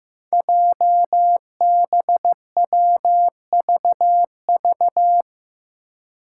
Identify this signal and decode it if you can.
Morse code, heard as a keyed tone, about 15 wpm, 700 Hz. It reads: JBWVV
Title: Five Challenging Letter-Number Combinations 15wpm